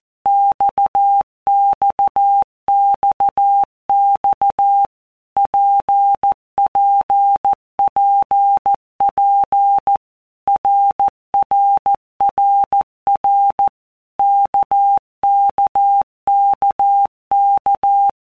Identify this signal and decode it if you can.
Morse code, heard as a keyed tone, about 14 wpm, 800 Hz. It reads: XXXX PPPP RRRR KKKK
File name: All Groups 1-minute Blocks / Groups@13 / RKPX